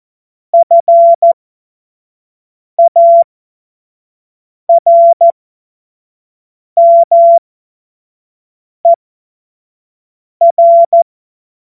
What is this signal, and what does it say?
Morse code, heard as a keyed tone, about 14 wpm, spaced as for 5 wpm, 680 Hz. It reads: FARMER
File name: Morsekurs / Witze / Witz 18. April 2023 25 F70